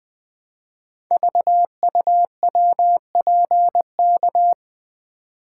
Morse code, heard as a keyed tone, about 20 wpm, 700 Hz. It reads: VUWPK
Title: Code Groups - 5 Letters - 20wpm